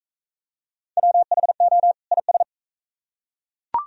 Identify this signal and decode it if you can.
Morse code, heard as a keyed tone, about 40 wpm, 700 Hz. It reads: WHO IS